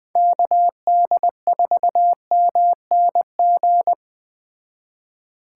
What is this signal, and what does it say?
Morse code, heard as a keyed tone, about 20 wpm, 700 Hz. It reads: KD4MNG
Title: Speed Racing - USA 2x3 Call Signs 17wpm